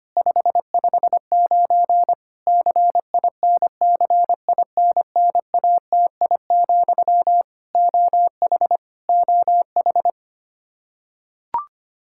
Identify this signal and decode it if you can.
Morse code, heard as a keyed tone, about 25 wpm, 700 Hz. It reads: 559 CINCINNATI, OH OH
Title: Practice QSO - Exchange 1 - Beginning - 25wpm